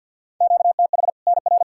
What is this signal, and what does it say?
Morse code, heard as a keyed tone, about 46 wpm, 700 Hz. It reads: QTH NR
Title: Practice QSO - Exchange 1 - Beginning - 50wpm